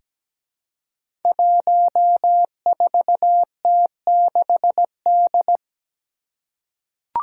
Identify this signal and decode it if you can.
Morse code, heard as a keyed tone, about 17 wpm, 700 Hz. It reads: 14T6D
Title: Five Challenging Letter-Number Combinations 17wpm